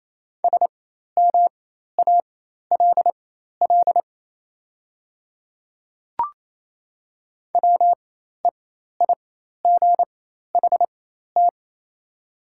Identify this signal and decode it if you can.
Morse code, heard as a keyed tone, about 28 wpm, spaced as for 13 wpm, 700 Hz. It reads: SMALL WEIGHT
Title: 4x Character Spacing - Top 500 Words - 28wpm